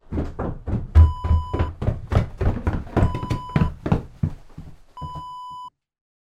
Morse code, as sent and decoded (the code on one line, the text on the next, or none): none